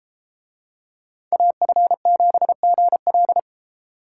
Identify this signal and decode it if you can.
Morse code, heard as a keyed tone, about 30 wpm, 700 Hz. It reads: AF7GL